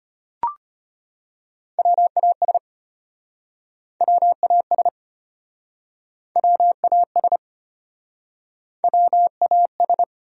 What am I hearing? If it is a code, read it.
Morse code, heard as a keyed tone, about 35 wpm, 700 Hz. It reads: WAS WAS WAS WAS